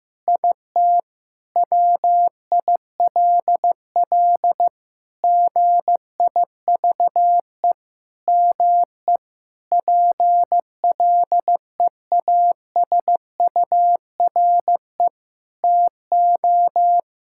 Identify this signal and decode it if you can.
Morse code, heard as a keyed tone, about 15 wpm, 700 Hz. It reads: IT WILL GIVE ME PLEASURE TO
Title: Sentences from Top 1600 Words 15wpm